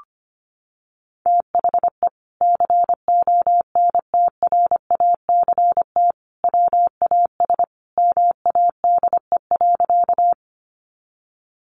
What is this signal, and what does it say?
Morse code, heard as a keyed tone, about 25 wpm, 700 Hz. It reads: THE CONTRACT WAS MADE.
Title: Sentences from Top 1400 Words 25wpm